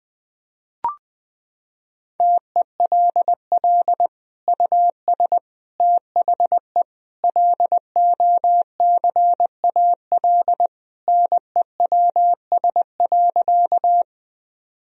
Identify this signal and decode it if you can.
Morse code, heard as a keyed tone, about 20 wpm, 700 Hz. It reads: TELL US THE LOCAL NEWS.
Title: Sentences from Top 1000 Words - Review Part 3 - 20wpm